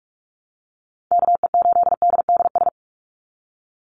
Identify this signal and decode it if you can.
Morse code, heard as a keyed tone, about 44 wpm, 700 Hz. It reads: KE8DDS
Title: USA Call Signs 45wpm